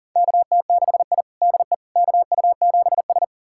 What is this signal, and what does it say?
Morse code, heard as a keyed tone, about 40 wpm, 700 Hz. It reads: KT6I DE KU7S
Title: USA Callsign DE Callsign - 4 Characters - 40wpm